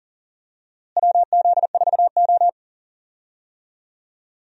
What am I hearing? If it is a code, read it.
Morse code, heard as a keyed tone, about 40 wpm, 700 Hz. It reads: WZ4O
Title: USA Call Signs - 4 Characters - Repeated 2x - 40wpm